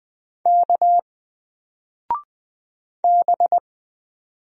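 Morse code, heard as a keyed tone, about 20 wpm, 700 Hz. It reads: K B
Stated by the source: Single letter and number Morse drill at 20 wpm, from N4LCD.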